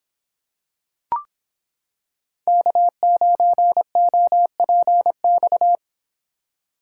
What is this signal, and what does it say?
Morse code, heard as a keyed tone, about 26 wpm, 700 Hz. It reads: K9OPX